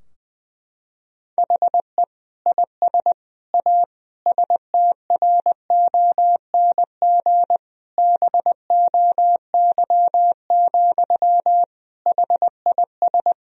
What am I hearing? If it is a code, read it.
Morse code, heard as a keyed tone, about 21 wpm, 700 Hz. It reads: HE IS A STRONG BOY, HIS
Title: Sentences from Top 1000 Words - Review Part 3 - 20wpm